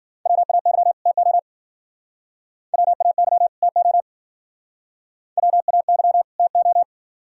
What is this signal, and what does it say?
Morse code, heard as a keyed tone, about 50 wpm, 700 Hz. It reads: WAY TO WAY TO WAY TO